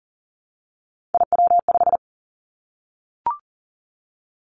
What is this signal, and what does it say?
Morse code, heard as a keyed tone, about 42 wpm, 700 Hz. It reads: IW5
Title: Three Challenging Letter-Number Combinations 40wpm